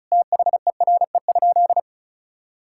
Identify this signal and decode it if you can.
Morse code, heard as a keyed tone, about 36 wpm, 700 Hz. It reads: THERE?
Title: Sentences from Top 1600 Words 35wpm